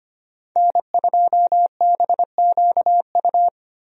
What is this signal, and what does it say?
Morse code, heard as a keyed tone, about 25 wpm, 700 Hz. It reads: N2BQU
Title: USA Call Signs - 5 Characters - Repeated 3x - 25wpm